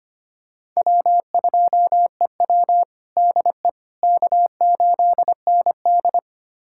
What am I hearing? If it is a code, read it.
Morse code, heard as a keyed tone, about 25 wpm, 700 Hz. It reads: W2EW DE K8ND
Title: USA Callsign DE Callsign - 4 Characters - 25wpm